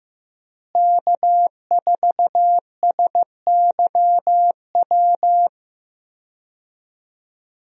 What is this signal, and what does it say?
Morse code, heard as a keyed tone, about 15 wpm, 700 Hz. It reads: K4SYW
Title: USA Call Signs - 5 Characters - Repeated 3x - 15wpm